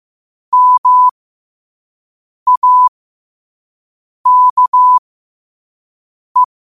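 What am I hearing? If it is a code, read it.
Morse code, heard as a keyed tone, about 15 wpm, spaced as for 5 wpm, 1000 Hz. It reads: MAKE
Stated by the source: Morse quote of the day at 5 wpm.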